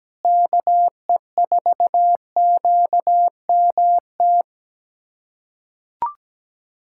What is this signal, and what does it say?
Morse code, heard as a keyed tone, about 17 wpm, 700 Hz. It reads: KE4QMT
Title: USA Callsign DE Callsign - 6 Characters - 17wpm